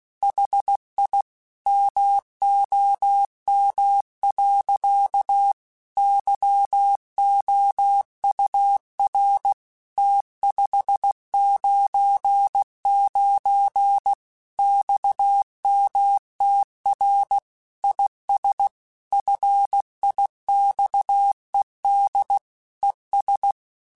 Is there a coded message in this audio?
Yes: Morse code from 0.2 to 23.5 s, heard as a keyed tone, about 16 wpm, 800 Hz. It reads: HI MOM. YOUR T599 XMTR IS FIXED ES